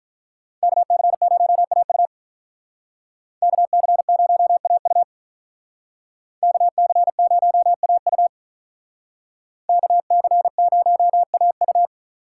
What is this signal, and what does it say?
Morse code, heard as a keyed tone, about 43 wpm, 700 Hz. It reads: KC0AU KC0AU KC0AU KC0AU
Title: Speed Racing - USA 2x2 Call Signs 35wpm